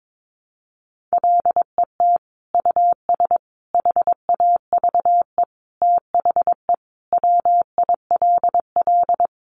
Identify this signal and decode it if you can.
Morse code, heard as a keyed tone, about 22 wpm, 700 Hz. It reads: LET US HAVE THE WILL
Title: Sentences from Top 1300 Words 22wpm